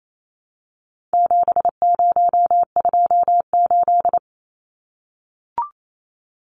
Morse code, heard as a keyed tone, about 28 wpm, 700 Hz. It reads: 7028
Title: Numbers - 4-Digit 28wpm